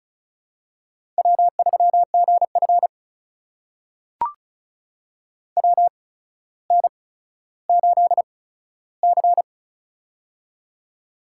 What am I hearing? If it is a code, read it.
Morse code, heard as a keyed tone, about 36 wpm, spaced as for 15 wpm, 700 Hz. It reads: W3GF WN8C
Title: USA Call Signs - 4 Characters - 8x Character Spacing - 35wpm